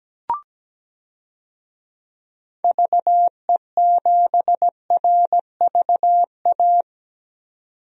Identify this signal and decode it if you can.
Morse code, heard as a keyed tone, about 17 wpm, 700 Hz. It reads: VE7RVA
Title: DXCC-100 Call Signs - 6 Characters - 17wpm